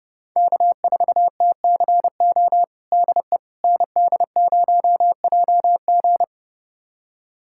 Morse code, heard as a keyed tone, about 30 wpm, 700 Hz. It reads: K4TCO DE ND0JG
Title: USA Callsign DE Callsign - 5 Characters - 30wpm